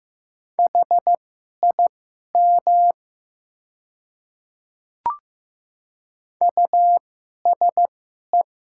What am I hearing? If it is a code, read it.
Morse code, heard as a keyed tone, about 15 wpm, spaced as for 11 wpm, 700 Hz. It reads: HIM USE
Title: 2x Character Spacing - Top 100 Words - 15wpm